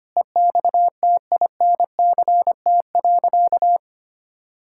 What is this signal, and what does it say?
Morse code, heard as a keyed tone, about 25 wpm, 700 Hz. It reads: EXTINCT.
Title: Pangram Challenge v2 - 25wpm